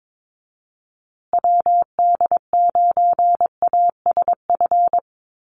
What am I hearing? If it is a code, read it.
Morse code, heard as a keyed tone, about 22 wpm, 700 Hz. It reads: WD9ASF
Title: USA Call Signs Repeated 2x 22wpm